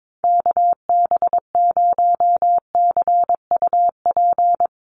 Morse code, heard as a keyed tone, about 22 wpm, 690 Hz. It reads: KB0CUP
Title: USA Call Signs Repeated 3x 22wpm